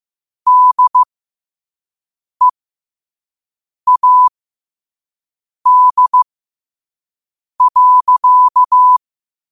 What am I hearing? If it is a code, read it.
Morse code, heard as a keyed tone, about 15 wpm, spaced as for 5 wpm, 1000 Hz. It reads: DEAD.
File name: qotdfiles / 5wpm